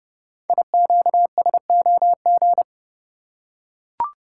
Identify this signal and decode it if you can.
Morse code, heard as a keyed tone, about 29 wpm, 690 Hz. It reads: IQSOG
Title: Code Groups - 5 Letters - 30wpm